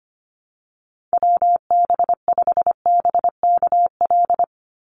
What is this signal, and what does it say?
Morse code, heard as a keyed tone, about 25 wpm, 700 Hz. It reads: WB5BKL
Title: NA SOTA Chaser Callsigns 25wpm